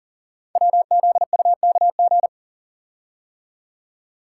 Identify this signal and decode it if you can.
Morse code, heard as a keyed tone, about 40 wpm, 690 Hz. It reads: WZUKG
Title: Code Groups - 5 Letters - 40wpm